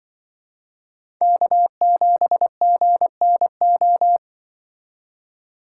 Morse code, heard as a keyed tone, about 25 wpm, 700 Hz. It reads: K7GNO
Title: Speed Racing - USA 1x3 Call Signs 20wpm